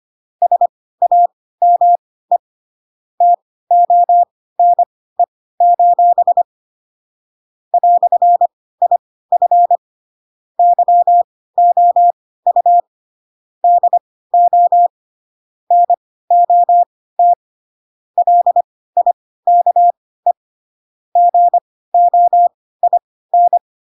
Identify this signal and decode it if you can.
Morse code, heard as a keyed tone, about 25 wpm, spaced as for 16 wpm, 700 Hz. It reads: SAME TONE: "IF YOU DO NOT LIKE GOIN
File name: Pinocchio / Pinocchio ch4 10 25 ik1hge